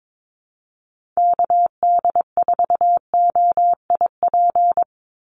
Morse code, heard as a keyed tone, about 22 wpm, 700 Hz. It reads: KD4OIP
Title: USA Call Signs 22wpm